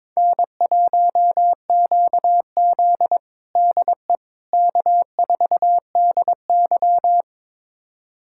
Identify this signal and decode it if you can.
Morse code, heard as a keyed tone, about 21 wpm, 700 Hz. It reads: N1QZ DE K4DY
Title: USA Callsign DE Callsign - 4 Characters - 22wpm